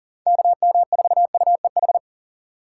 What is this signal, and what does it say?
Morse code, heard as a keyed tone, about 40 wpm, 700 Hz. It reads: KM4UEH
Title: USA Call Signs - 4x Character Spacing - 40wpm